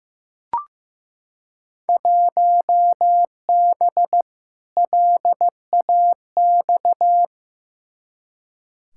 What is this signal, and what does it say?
Morse code, heard as a keyed tone, about 16 wpm, 700 Hz. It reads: 1B LAX
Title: Field Day Exchanges - 15wpm